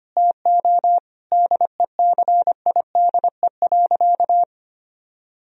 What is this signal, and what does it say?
Morse code, heard as a keyed tone, about 25 wpm, 700 Hz. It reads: TO DECIDE.
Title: Sentences from Top 1000 Words - Review Part 3 - 25wpm